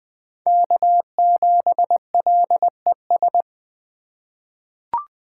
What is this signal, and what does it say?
Morse code, heard as a keyed tone, about 21 wpm, 700 Hz. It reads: K7LES
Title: USA Call Signs - 3x Character Spacing - 20wpm